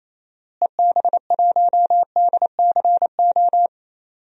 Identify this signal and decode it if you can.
Morse code, heard as a keyed tone, about 28 wpm, 700 Hz. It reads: EB1DCO
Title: DXCC-100 Call Signs - 6 Characters - Repeated 3x - 28wpm